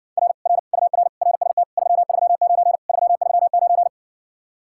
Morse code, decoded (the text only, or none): R R UR RST 339 339